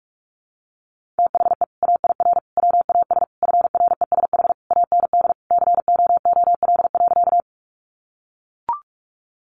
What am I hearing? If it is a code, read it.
Morse code, heard as a keyed tone, about 44 wpm, 700 Hz. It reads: THE AIR WAS FRESH AND COOL.
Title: Sentences from Top 1200 Words 45wpm